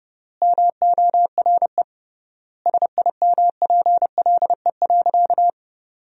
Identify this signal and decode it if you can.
Morse code, heard as a keyed tone, about 28 wpm, 700 Hz. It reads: MORE SIMPLE.
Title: 3x Word-Spacing - Sentences from Top 500 Words - 30wpm